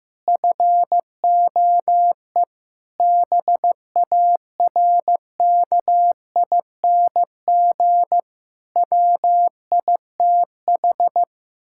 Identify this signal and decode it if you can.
Morse code, heard as a keyed tone, about 15 wpm, 700 Hz. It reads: FOE BARKING WITH
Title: Pangram Challenge v2 - 15wpm